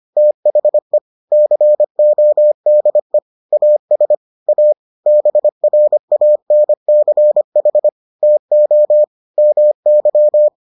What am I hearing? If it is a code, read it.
Morse code, heard as a keyed tone, about 25 wpm, 600 Hz. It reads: THE CODE AS A BRANCH TO MY